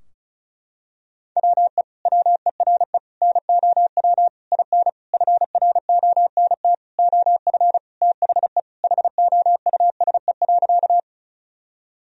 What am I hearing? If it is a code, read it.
Morse code, heard as a keyed tone, about 36 wpm, 700 Hz. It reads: WE WERE NOW IN FRONT OF THE HOUSE.
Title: Sentences from Top 1000 Words - Review Part 2 - 35wpm